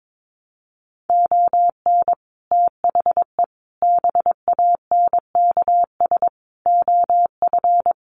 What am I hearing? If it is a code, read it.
Morse code, heard as a keyed tone, about 22 wpm, 700 Hz. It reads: ON THE BANKS OF
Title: Sets of 5 Words Encore2 22wpm